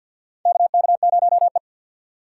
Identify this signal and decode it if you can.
Morse code, heard as a keyed tone, about 50 wpm, 700 Hz. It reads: KK0E
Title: USA Call Signs - 4 Characters - 2x Character Spacing - 50wpm